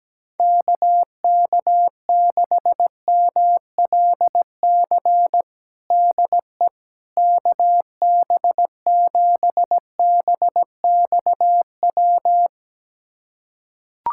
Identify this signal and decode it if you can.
Morse code, heard as a keyed tone, about 17 wpm, 700 Hz. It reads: KK6MLC DE KB7BXW
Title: USA Callsign DE Callsign 17wpm